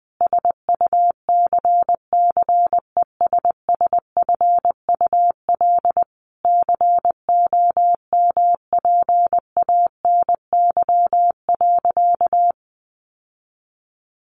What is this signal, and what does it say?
Morse code, heard as a keyed tone, about 19 wpm, 700 Hz. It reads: SUCCESSFUL COMPANY.
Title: Sentences from Top 1100 Words 20wpm